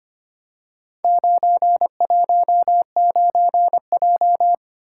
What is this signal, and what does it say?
Morse code, heard as a keyed tone, about 25 wpm, 700 Hz. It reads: 919J
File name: practice / 25wpm / Four Letter-Number Combinations - 25wpm